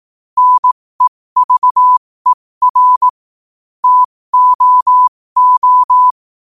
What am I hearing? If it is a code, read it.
Morse code, heard as a keyed tone, about 18 wpm, spaced as for 15 wpm, 1000 Hz. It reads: NEVER TOO
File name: qotdfiles / 15wpm